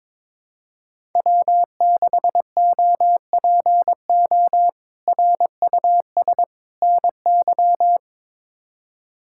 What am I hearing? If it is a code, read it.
Morse code, heard as a keyed tone, about 21 wpm, 700 Hz. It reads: W6OPO RUS NY
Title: K1USN SST QSO Practice - 22wpm